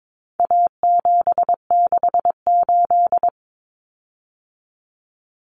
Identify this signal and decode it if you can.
Morse code, heard as a keyed tone, about 22 wpm, 700 Hz. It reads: A768
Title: Common Cut Numbers - 4-Digit 22wpm